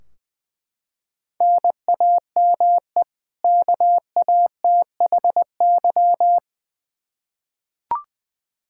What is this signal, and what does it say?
Morse code, heard as a keyed tone, about 20 wpm, 700 Hz. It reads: NAME KATHY
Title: Practice QSO - Exchange 1 - Beginning - 20wpm